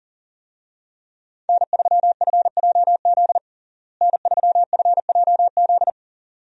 Morse code, heard as a keyed tone, about 37 wpm, 700 Hz. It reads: N3FJZ N3FJZ
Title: USA Call Signs - 5 Characters - Repeated 2x - 40wpm